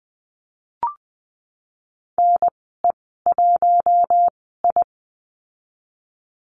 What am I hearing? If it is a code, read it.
Morse code, heard as a keyed tone, about 20 wpm, spaced as for 15 wpm, 700 Hz. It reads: NE1I
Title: 128 Lesson 21g - Number 6 - Call Signs 20wpm